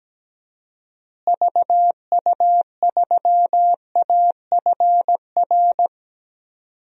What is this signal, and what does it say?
Morse code, heard as a keyed tone, about 17 wpm, 700 Hz. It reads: VU3AFR
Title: DXCC-100 Call Signs - 6 Characters - Repeated 3x - 17wpm